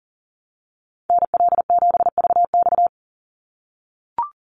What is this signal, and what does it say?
Morse code, heard as a keyed tone, about 40 wpm, 700 Hz. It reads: NL7VX
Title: Call Signs - 40wpm